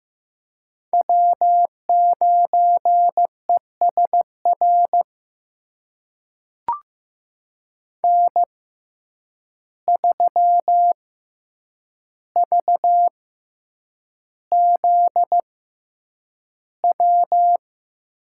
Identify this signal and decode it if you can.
Morse code, heard as a keyed tone, about 15 wpm, spaced as for 8 wpm, 700 Hz. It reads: W9ESR N3VZW
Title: USA Call Signs - 5 Characters - 6x Character Spacing - 15wpm